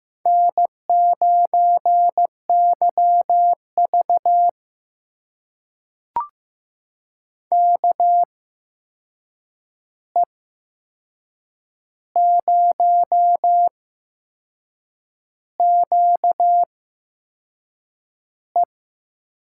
Morse code, heard as a keyed tone, about 15 wpm, spaced as for 6 wpm, 700 Hz. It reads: N9YV KE0QE
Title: USA Call Signs - 8x Character Spacing - 15wpm